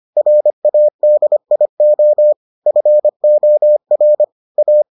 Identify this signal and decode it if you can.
Morse code, heard as a keyed tone, about 25 wpm, 600 Hz. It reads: RADIO FOR A